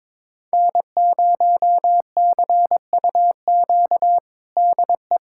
Morse code, heard as a keyed tone, about 22 wpm, 700 Hz. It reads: N0CUQ DE
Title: USA Callsign DE Callsign - 5 Characters - 22wpm